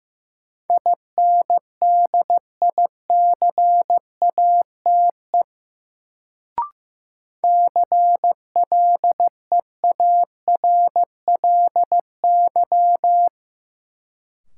Morse code, heard as a keyed tone, about 16 wpm, 700 Hz. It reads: INDICATE CLEARLY
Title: Sentences from Top 1300 Words - New Word Drill - 15wpm